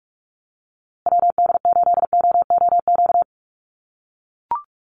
Morse code, heard as a keyed tone, about 44 wpm, 700 Hz. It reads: WD8OOQ
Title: USA Call Signs Repeated 2x 45wpm